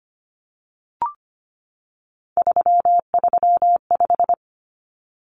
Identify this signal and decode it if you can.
Morse code, heard as a keyed tone, about 25 wpm, 700 Hz. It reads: 335
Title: Numbers - 3-Digit 25wpm